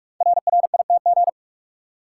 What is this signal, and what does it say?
Morse code, heard as a keyed tone, about 44 wpm, 700 Hz. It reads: RRITG
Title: Code Groups - 5 Letter:Numbers - 45wpm